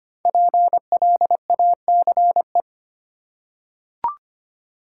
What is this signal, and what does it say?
Morse code, heard as a keyed tone, about 25 wpm, 700 Hz. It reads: PLACE
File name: 5 Letter Words - First 3 Letters Given - 25wpm